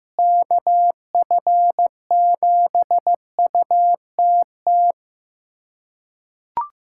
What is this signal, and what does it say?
Morse code, heard as a keyed tone, about 15 wpm, 700 Hz. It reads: KF7UTT
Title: USA Callsign DE Callsign 15wpm